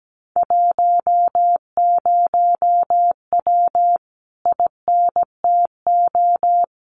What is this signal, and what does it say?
Morse code, heard as a keyed tone, about 17 wpm, 700 Hz. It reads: 10W INTO
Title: Common QSO Phrases Encore 17wpm